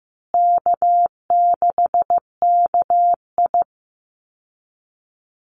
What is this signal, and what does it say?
Morse code, heard as a keyed tone, about 15 wpm, 700 Hz. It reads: K6KI
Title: Call Signs Repeated 2x - 15wpm